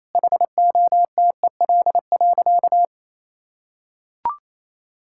Morse code, heard as a keyed tone, about 28 wpm, 700 Hz. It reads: HOTEL.
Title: Sentences from Top 1600 Words 28wpm